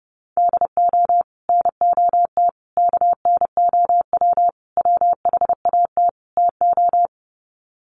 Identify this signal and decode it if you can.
Morse code, heard as a keyed tone, about 28 wpm, 700 Hz. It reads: DO NOT KNOW WHAT TO
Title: Sets of 5 Words 30wpm